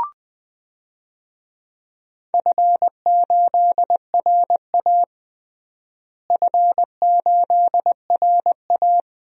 Morse code, heard as a keyed tone, about 20 wpm, 700 Hz. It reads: F8RA F8RA
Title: DXCC-100 Call Signs - 4 Characters - Repeated 3x - 20wpm